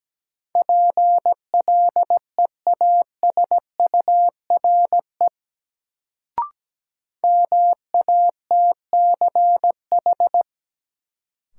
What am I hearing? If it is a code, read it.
Morse code, heard as a keyed tone, about 17 wpm, 700 Hz. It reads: PLEASURE MATCH
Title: Sentences from Top 1600 Words - New Word Drill Alt - 17wpm